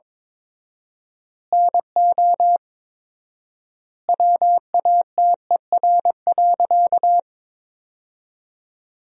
Morse code, heard as a keyed tone, about 21 wpm, 700 Hz. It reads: NO WATER.
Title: 4x Word-Spacing - Sentences from Top 100 Words - 22wpm